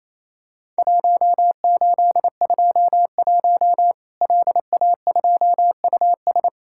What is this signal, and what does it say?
Morse code, heard as a keyed tone, about 27 wpm, 700 Hz. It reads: 1821 LA2US
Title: Worked All Europe DX Contest - QTC Exchange - 28wpm